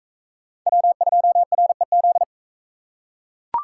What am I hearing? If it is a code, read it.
Morse code, heard as a keyed tone, about 40 wpm, 700 Hz. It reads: W2REZ